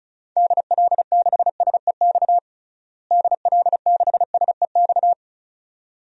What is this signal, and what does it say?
Morse code, heard as a keyed tone, about 36 wpm, 700 Hz. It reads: DL6SEX DL6SEX
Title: DXCC-100 Call Signs - 6 Characters - Repeated 2x - 35wpm